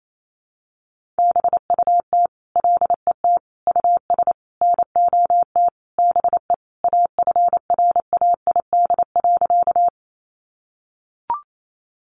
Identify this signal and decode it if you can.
Morse code, heard as a keyed tone, about 28 wpm, 700 Hz. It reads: BUT LET US NOT BE AFRAID.
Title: Sentences from Top 1500 Words 28wpm